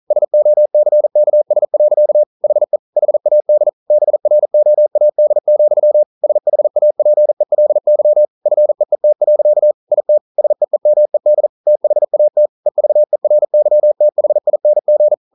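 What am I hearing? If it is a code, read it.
Morse code, heard as a keyed tone, about 42 wpm, 600 Hz. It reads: SOCKS. HE HAD BROAD, SHAPELY FEET. IT SEEMED THAT EVERYTHING
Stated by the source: Morse code practice at 41 wpm, from CW med Gnister.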